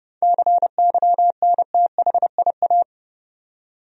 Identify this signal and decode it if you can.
Morse code, heard as a keyed tone, about 30 wpm, 700 Hz. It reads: CYNTHIA
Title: USA Common Names - 30wpm